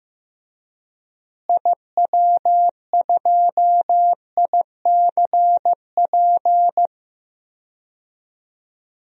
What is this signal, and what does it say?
Morse code, heard as a keyed tone, about 15 wpm, 700 Hz. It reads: IW2ICP
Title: DXCC-100 Call Signs - 6 Characters - Repeated 2x - 15wpm